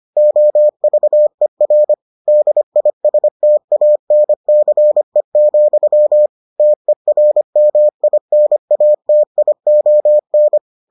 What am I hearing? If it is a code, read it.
Morse code, heard as a keyed tone, about 25 wpm, 600 Hz. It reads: OVER DISTANCE, TERMINATION